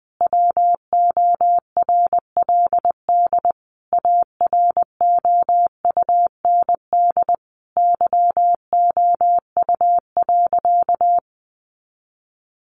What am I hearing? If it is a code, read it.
Morse code, heard as a keyed tone, about 20 wpm, 700 Hz. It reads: WORLD AROUND YOU.
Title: Sentences from Top 1200 Words 20wpm